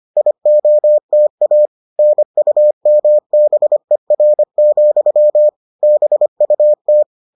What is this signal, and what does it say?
Morse code, heard as a keyed tone, about 25 wpm, 600 Hz. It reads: IOTA NUMBER, BUT